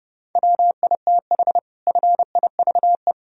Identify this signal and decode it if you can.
Morse code, heard as a keyed tone, about 32 wpm, 700 Hz. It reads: WITH FIVE
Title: Sentences from Top 1000 Words - Review Part 3 - 30wpm